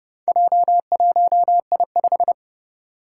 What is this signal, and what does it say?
Morse code, heard as a keyed tone, about 31 wpm, 700 Hz. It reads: J1I5
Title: Four Challenging Letter-Number Combinations 30wpm